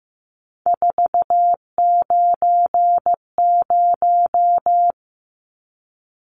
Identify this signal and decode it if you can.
Morse code, heard as a keyed tone, about 15 wpm, 700 Hz. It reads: 490
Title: Numbers - 3-Digit 15wpm